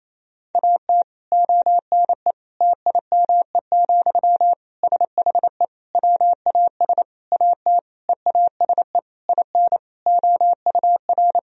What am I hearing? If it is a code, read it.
Morse code, heard as a keyed tone, about 28 wpm, 700 Hz. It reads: AT ONE TIME, SHE WAS AT EASE IN OUR
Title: Sentences from Top 1000 Words - Review Part 1 - 28wpm